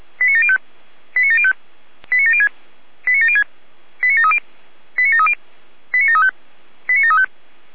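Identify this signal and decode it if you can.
Five-tone selective call (ZVEI), heard as quick runs of five tones, tones of about 70 ms. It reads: 89705 89705 89706 89706 89730 89730 89735 89735
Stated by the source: ZVEI five-tone selective call received with an SDR.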